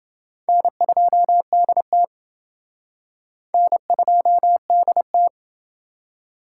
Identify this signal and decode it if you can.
Morse code, heard as a keyed tone, about 30 wpm, 700 Hz. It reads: N2DT N2DT